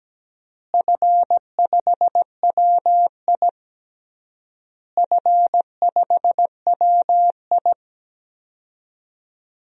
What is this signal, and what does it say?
Morse code, heard as a keyed tone, about 17 wpm, 700 Hz. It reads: F5WI F5WI
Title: DXCC-100 Call Signs - 4 Characters - Repeated 3x - 17wpm